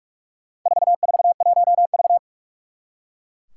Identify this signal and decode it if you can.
Morse code, heard as a keyed tone, about 44 wpm, 700 Hz. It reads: 441V
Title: Four Challenging Letter-Number Combinations 45wpm